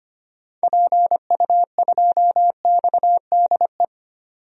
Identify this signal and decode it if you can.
Morse code, heard as a keyed tone, about 25 wpm, 700 Hz. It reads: PU2XDE
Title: DXCC-100 Call Signs - 6 Characters - 25wpm